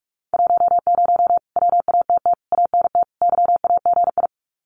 Morse code, heard as a keyed tone, about 44 wpm, 700 Hz. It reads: 10 WATT ANT YAGI
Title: Common QSO Phrases Encore 45wpm